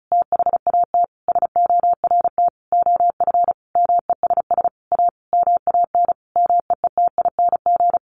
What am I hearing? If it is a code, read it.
Morse code, heard as a keyed tone, about 34 wpm, 700 Hz. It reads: THAT SORT OF MESS A MAN MEETING
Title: A Sound Of Thunder 35wpm